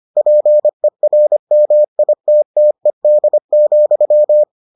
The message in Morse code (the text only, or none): PERMITTED,